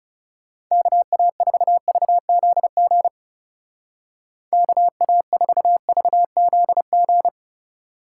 Morse code, heard as a keyed tone, about 38 wpm, 700 Hz. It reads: KA4VZG KA4VZG